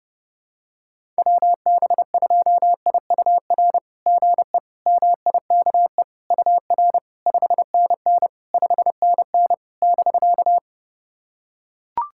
Morse code, heard as a keyed tone, about 32 wpm, 690 Hz. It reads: WB2IUR GE MIKE UR 5NN 5NN <BK>
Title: SOTA QSO Practice 30wpm